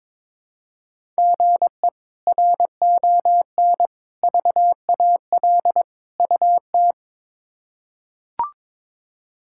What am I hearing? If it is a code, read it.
Morse code, heard as a keyed tone, about 22 wpm, 700 Hz. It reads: GE RON VAL UT
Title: K1USN SST QSO Practice - 22wpm